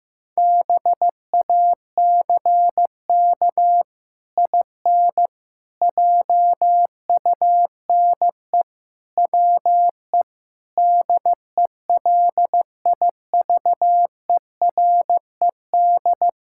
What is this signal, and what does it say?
Morse code, heard as a keyed tone, about 16 wpm, 700 Hz. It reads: BACK IN JUNE WE DELIVERED
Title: Pangram Challenge v2 - 15wpm